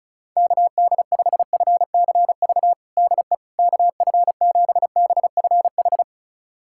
Morse code, heard as a keyed tone, about 36 wpm, 700 Hz. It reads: KD5FCV DE KF7BFH
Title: USA Callsign DE Callsign 35wpm